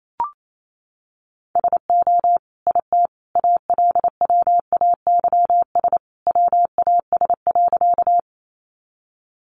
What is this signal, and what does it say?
Morse code, heard as a keyed tone, about 28 wpm, 700 Hz. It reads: SO IT ALWAYS WAS.
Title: Sentences from Top 300 Words 28wpm